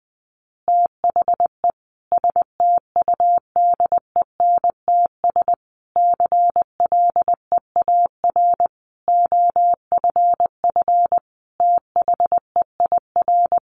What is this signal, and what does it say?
Morse code, heard as a keyed tone, about 20 wpm, 700 Hz. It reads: THE STUDENTS CLEAR OFF THEIR
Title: Additional Easy Phrases 20wpm